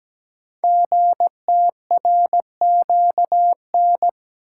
Morse code, heard as a keyed tone, about 17 wpm, 700 Hz. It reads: GTRQN
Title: Code Groups - 5 Letters - 17wpm